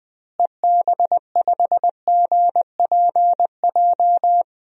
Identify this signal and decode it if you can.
Morse code, heard as a keyed tone, about 20 wpm, 700 Hz. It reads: EB5GPJ
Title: DXCC-100 Call Signs - 6 Characters - Repeated 2x - 20wpm